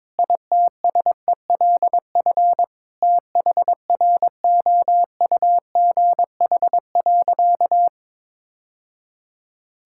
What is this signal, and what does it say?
Morse code, heard as a keyed tone, about 22 wpm, 700 Hz. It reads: ITSELF THROUGH.
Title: Sentences from Top 1000 Words - Review Part 5 - 22wpm